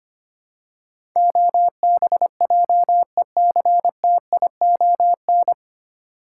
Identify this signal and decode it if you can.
Morse code, heard as a keyed tone, about 25 wpm, 700 Hz. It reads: OBJECTION
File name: Nine Letter Words - 25wpm